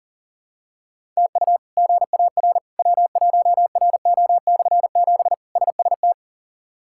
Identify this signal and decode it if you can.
Morse code, heard as a keyed tone, about 40 wpm, 700 Hz. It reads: TU GAR W1RO/7 SST
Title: K1USN SST QSO Practice - 40wpm